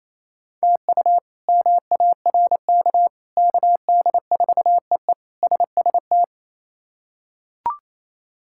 Morse code, heard as a keyed tone, about 27 wpm, 700 Hz. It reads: TU MARK KD4EE SST
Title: K1USN SST QSO Practice - 28wpm